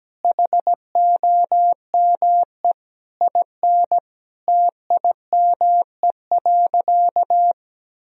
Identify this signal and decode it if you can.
Morse code, heard as a keyed tone, about 17 wpm, 700 Hz. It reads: HOME IN TIME.
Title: Sentences from Top 1000 Words - Review Part 3 - 17wpm